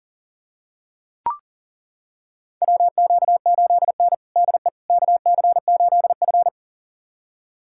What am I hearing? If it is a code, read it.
Morse code, heard as a keyed tone, about 40 wpm, 700 Hz. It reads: WQ8N DE KC8F
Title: USA Callsign DE Callsign - 4 Characters - 40wpm